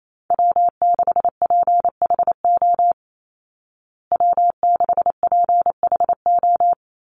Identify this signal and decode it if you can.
Morse code, heard as a keyed tone, about 28 wpm, 700 Hz. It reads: W6PHO W6PHO
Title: Call Signs Repeated 3x - 28wpm